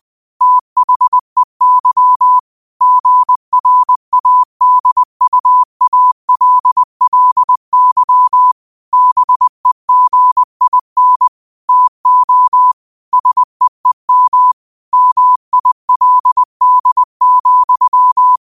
Morse code, heard as a keyed tone, about 20 wpm, 1000 Hz. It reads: THEY GRADUALLY BEGIN TO SEEM MILD,
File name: qotdfiles / 20wpm